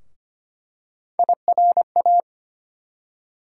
Morse code, heard as a keyed tone, about 25 wpm, 700 Hz. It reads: IRA
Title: USA Call Sign Postfixes - 3 Letters 25wpm